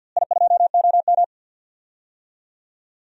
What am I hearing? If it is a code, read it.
Morse code, heard as a keyed tone, about 49 wpm, 700 Hz. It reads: I2OM
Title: DXCC-100 Call Signs - 4 Characters - Repeated 3x - 50wpm